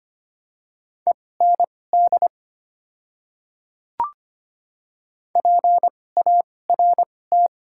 Morse code, heard as a keyed tone, about 25 wpm, spaced as for 18 wpm, 700 Hz. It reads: END PART
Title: 2x Character Spacing - Top 200 Words - 25wpm